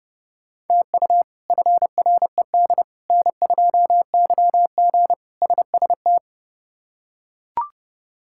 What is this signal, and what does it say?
Morse code, heard as a keyed tone, about 30 wpm, 700 Hz. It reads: TU FRED N2YG SST
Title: K1USN SST QSO Practice - 30wpm